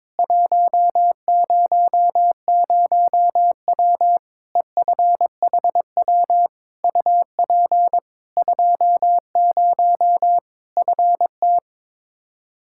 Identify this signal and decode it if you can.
Morse code, heard as a keyed tone, about 22 wpm, 700 Hz. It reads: 100W EFHW UP 20 FT
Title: Practice QSO - Exchange 2 - Info - 22wpm